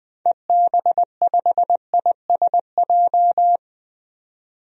Morse code, heard as a keyed tone, about 20 wpm, 700 Hz. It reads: EB5ISJ
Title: DXCC-100 Call Signs - 6 Characters - 20wpm